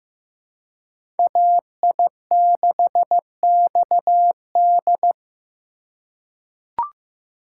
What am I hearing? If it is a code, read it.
Morse code, heard as a keyed tone, about 15 wpm, 700 Hz. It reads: AI6XD
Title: USA Call Signs - 5 Characters - 2x Character Spacing - 15wpm